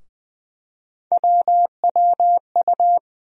Morse code, heard as a keyed tone, about 20 wpm, 700 Hz. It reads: WWU
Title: USA Call Sign Postfixes - 3 Letters 20wpm